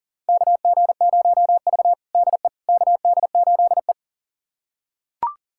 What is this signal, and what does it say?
Morse code, heard as a keyed tone, about 40 wpm, 700 Hz. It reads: KG0V DE KD8E
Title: USA Callsign DE Callsign - 4 Characters - 40wpm